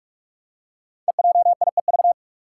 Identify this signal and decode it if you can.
Morse code, heard as a keyed tone, about 44 wpm, 700 Hz. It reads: EJIEV
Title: Five Challenging Letter-Number Combinations 45wpm